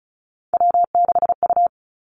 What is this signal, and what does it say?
Morse code, heard as a keyed tone, about 36 wpm, 700 Hz. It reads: W6U
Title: Three Challenging Letter-Number Combinations 35wpm